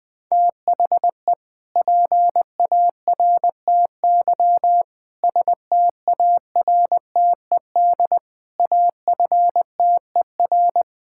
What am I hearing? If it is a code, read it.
Morse code, heard as a keyed tone, about 20 wpm, 700 Hz. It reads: THE PARTY STARTED AFTER
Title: Sentences with 2-Word Phrases - 20wpm